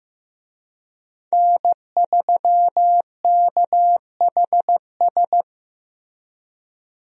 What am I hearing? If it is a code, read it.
Morse code, heard as a keyed tone, about 15 wpm, 700 Hz. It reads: N3KHS
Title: USA Call Signs - 5 Characters - Repeated 3x - 15wpm